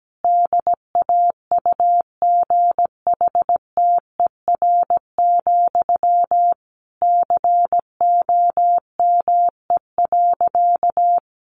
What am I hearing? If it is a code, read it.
Morse code, heard as a keyed tone, about 17 wpm, 700 Hz. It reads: DAUGHTER, COME.
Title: Sentences from Top 1500 Words 17wpm